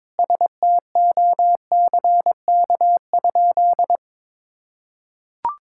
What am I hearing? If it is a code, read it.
Morse code, heard as a keyed tone, about 22 wpm, 700 Hz. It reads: STOCK?
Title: Sentences from Top 1000 Words - Review Part 5 - 22wpm